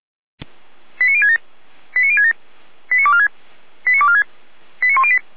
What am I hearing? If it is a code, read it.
Five-tone selective call (ZVEI), heard as quick runs of five tones, tones of about 70 ms. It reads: 89067 89067 89246 89246 89108